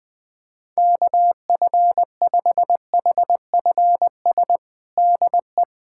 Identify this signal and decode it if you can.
Morse code, heard as a keyed tone, about 21 wpm, 700 Hz. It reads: KF5HFS DE
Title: USA Callsign DE Callsign - 6 Characters - 20wpm